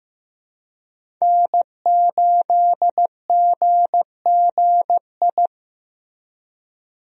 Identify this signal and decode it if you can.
Morse code, heard as a keyed tone, about 16 wpm, 700 Hz. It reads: N8GGI
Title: USA Call Signs - 5 Characters - Repeated 2x - 15wpm